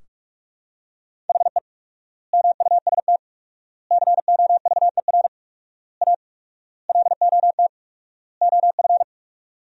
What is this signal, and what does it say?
Morse code, heard as a keyed tone, about 41 wpm, 700 Hz. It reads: HE MUST COVER A LOT OF
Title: 4x Word-Spacing - Sentences from Top 500 Words - 45wpm